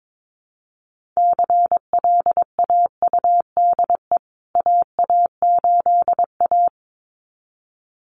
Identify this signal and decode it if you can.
Morse code, heard as a keyed tone, about 22 wpm, 700 Hz. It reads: CLAUDE AA8A
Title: CWT Exchange 22wpm